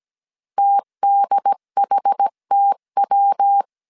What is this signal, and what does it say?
Morse code, heard as a keyed tone, about 17 wpm, 800 Hz. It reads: TDHTW
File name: CW / DET / 1 3 1500C S A 16wpm 2141